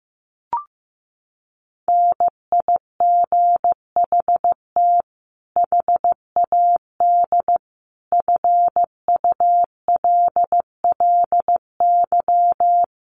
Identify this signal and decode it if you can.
Morse code, heard as a keyed tone, about 15 wpm, 700 Hz. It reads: NIGHT HAD FULLY
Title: Sentences from Top 1300 Words 15wpm